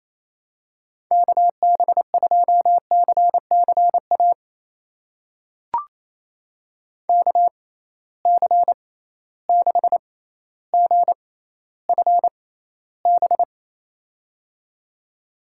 Morse code, heard as a keyed tone, about 28 wpm, spaced as for 14 wpm, 700 Hz. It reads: KB2CCA KC6GFB
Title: USA Call Signs - 6 Characters - 6x Character Spacing - 28wpm